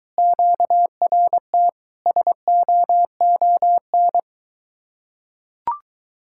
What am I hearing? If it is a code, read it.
Morse code, heard as a keyed tone, about 24 wpm, 700 Hz. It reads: QRT SOON